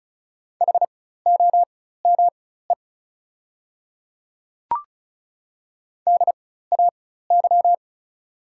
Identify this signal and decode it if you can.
Morse code, heard as a keyed tone, about 35 wpm, spaced as for 16 wpm, 700 Hz. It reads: HOME DAY
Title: 4x Character Spacing - Top 300 Words - 35wpm